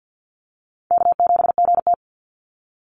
Morse code, heard as a keyed tone, about 49 wpm, 700 Hz. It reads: K7GT
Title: NA SOTA Chaser Callsigns 50wpm